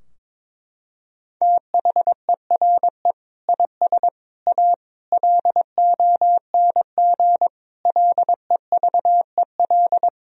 Morse code, heard as a keyed tone, about 22 wpm, 700 Hz. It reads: THERE IS A LONG LEVEL
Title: Sentences from Top 1000 Words - Review Part 4 - 22wpm